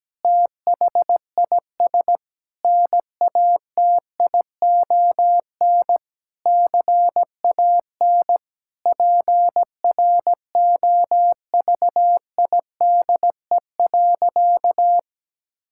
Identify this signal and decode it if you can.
Morse code, heard as a keyed tone, about 17 wpm, 700 Hz. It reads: THIS NATION CAN PROVIDE.
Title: Sentences from Top 1000 Words - Review Part 6 - 17wpm